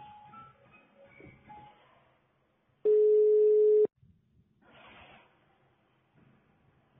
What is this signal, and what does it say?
Ringback tone (Europe, 425 Hz).
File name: sauvegarde / ACRRecordings- -jusquau 17-juin-2017 / Mme ???, Rue Arthur Galant Wanze. 32 Rue Du Vieux Pont Huy Vers Wanze 2017 05 21 16 20 02